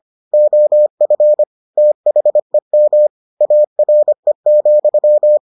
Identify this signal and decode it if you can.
Morse code, heard as a keyed tone, about 25 wpm, 600 Hz. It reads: OF THEM ARE,